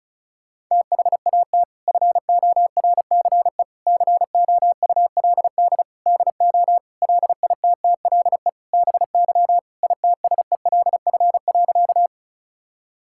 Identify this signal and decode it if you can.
Morse code, heard as a keyed tone, about 36 wpm, 700 Hz. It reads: THAT FORCE COULD DO LITTLE BY ITSELF.
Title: Sentences from Top 1000 Words - Review Part 5 - 35wpm